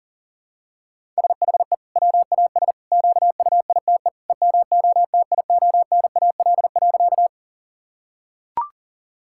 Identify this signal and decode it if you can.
Morse code, heard as a keyed tone, about 40 wpm, 700 Hz. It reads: SHE WAS QUITE EMOTIONAL.
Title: Sentences from Top 1000 Words - Review Part 5 - 40wpm